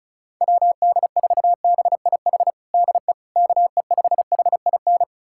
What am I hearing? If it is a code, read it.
Morse code, heard as a keyed tone, about 36 wpm, 700 Hz. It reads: WD4BIH DE KE5HIN
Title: USA Callsign DE Callsign 35wpm